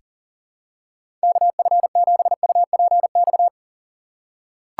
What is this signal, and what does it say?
Morse code, heard as a keyed tone, about 40 wpm, 700 Hz. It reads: KF7UPX
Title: USA Call Signs - 6x Character Spacing - 40wpm